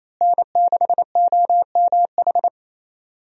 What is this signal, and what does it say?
Morse code, heard as a keyed tone, about 27 wpm, 700 Hz. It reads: N6OMH
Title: USA Call Signs - 5 Characters - Repeated 3x - 28wpm